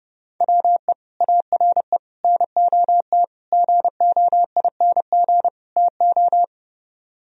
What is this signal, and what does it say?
Morse code, heard as a keyed tone, about 30 wpm, 700 Hz. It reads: WE ARE NOT GOING TO